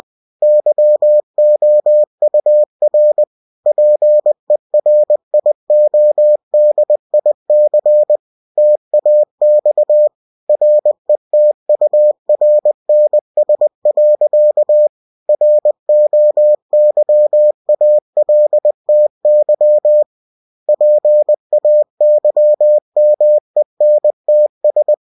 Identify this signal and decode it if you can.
Morse code, heard as a keyed tone, about 20 wpm, 600 Hz. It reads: YOUR PERIODIC TAX RETURNS. ROYALTY PAYMENTS
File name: cw-books / winnie-the-pooh / winnie-20wpm-page-0117